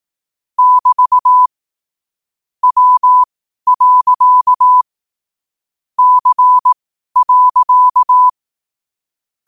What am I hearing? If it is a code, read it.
Morse code, heard as a keyed tone, about 18 wpm, spaced as for 12 wpm, 1000 Hz. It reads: <BT> W. C.
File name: qotdfiles / 12wpm